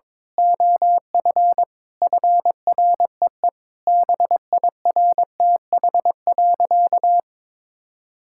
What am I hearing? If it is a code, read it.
Morse code, heard as a keyed tone, about 21 wpm, 700 Hz. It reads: OF FREE BIRTH.
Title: Sentences from Top 1000 Words - Review Part 5 - 22wpm